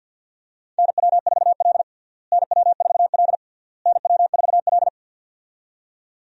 Morse code, heard as a keyed tone, about 52 wpm, 700 Hz. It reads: NW4L NW4L NW4L
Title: USA Call Signs - 4 Characters - Repeated 3x - 50wpm